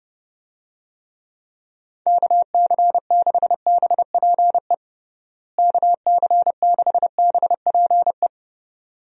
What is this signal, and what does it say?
Morse code, heard as a keyed tone, about 32 wpm, 700 Hz. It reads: KC6BPE KC6BPE
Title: USA Call Signs - 6 Characters - Repeated 2x - 30wpm